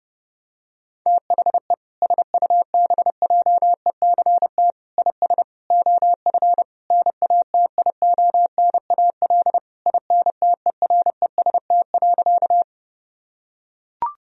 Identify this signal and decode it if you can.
Morse code, heard as a keyed tone, about 30 wpm, 700 Hz. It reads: THE SUBJECT IS OF NATIONAL INTEREST.
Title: Sentences from Top 1000 Words - Review Part 4 - 30wpm